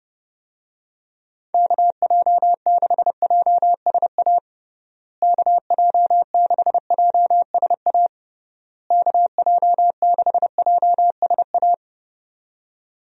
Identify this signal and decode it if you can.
Morse code, heard as a keyed tone, about 30 wpm, 700 Hz. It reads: KJ6JSA KJ6JSA KJ6JSA
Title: USA Call Signs - 6 Characters - Repeated 3x - 30wpm